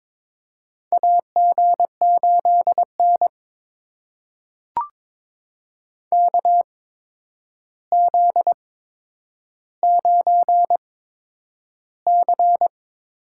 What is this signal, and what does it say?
Morse code, heard as a keyed tone, about 22 wpm, spaced as for 9 wpm, 700 Hz. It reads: AG8N KZ9C
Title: USA Call Signs - 4 Characters - 8x Character Spacing - 22wpm